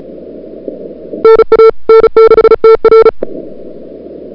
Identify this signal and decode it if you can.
Morse code, heard as a keyed tone, about 35 wpm, 440 Hz. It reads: NA N6TR
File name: CWSprint / February 2007 CW Sprint / 01-2007 CW Sprint Hour1